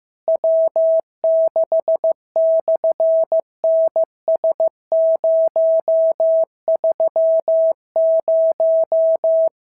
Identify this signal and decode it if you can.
Morse code, heard as a keyed tone, about 15 wpm, 650 Hz. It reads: W6/NS030
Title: SOTA QSO Practice 15wpm